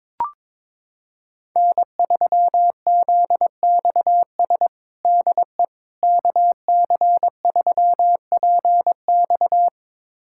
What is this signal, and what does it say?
Morse code, heard as a keyed tone, about 22 wpm, 700 Hz. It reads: N3ZXS DE KC3PX
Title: USA Callsign DE Callsign - 5 Characters - 22wpm